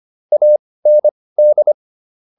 Morse code, heard as a keyed tone, about 25 wpm, spaced as for 18 wpm, 600 Hz. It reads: AND